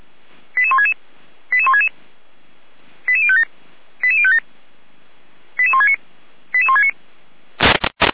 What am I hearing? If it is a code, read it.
Five-tone selective call (ZVEI), heard as quick runs of five tones, tones of about 70 ms. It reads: 88166 88166 88057 88057 88169 88169